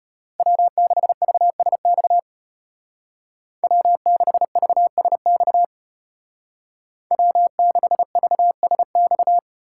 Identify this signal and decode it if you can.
Morse code, heard as a keyed tone, about 33 wpm, 700 Hz. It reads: W6VSX W6VSX W6VSX